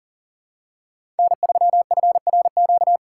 Morse code, heard as a keyed tone, about 40 wpm, 700 Hz. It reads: N3FRQ
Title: USA Call Signs - 5 Characters - 8x Character Spacing - 40wpm